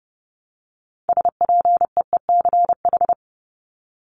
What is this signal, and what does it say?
Morse code, heard as a keyed tone, about 30 wpm, 700 Hz. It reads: SPEECH
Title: Top 1751-2000 Words 30wpm